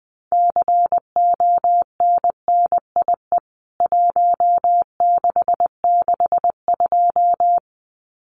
Morse code, heard as a keyed tone, about 20 wpm, 700 Hz. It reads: CONNIE 1662
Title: CWT Exchange 20wpm